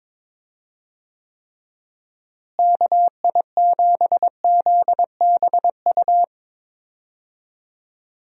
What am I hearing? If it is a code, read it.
Morse code, heard as a keyed tone, about 21 wpm, 700 Hz. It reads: KI7ZBU
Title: USA Call Signs - 6 Characters - Repeated 3x - 22wpm